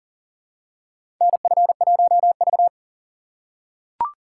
Morse code, heard as a keyed tone, about 40 wpm, 700 Hz. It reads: NF1V
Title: USA Call Signs - 4 Characters - 4x Character Spacing - 40wpm